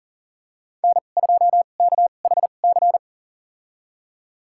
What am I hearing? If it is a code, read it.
Morse code, heard as a keyed tone, about 40 wpm, spaced as for 29 wpm, 700 Hz. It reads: N2KHC
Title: USA Call Signs - 2x Character Spacing - 40wpm